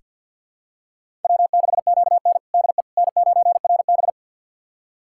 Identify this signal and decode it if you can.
Morse code, heard as a keyed tone, about 48 wpm, 700 Hz. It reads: W6QN DE N9RB
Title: USA Callsign DE Callsign - 4 Characters - 50wpm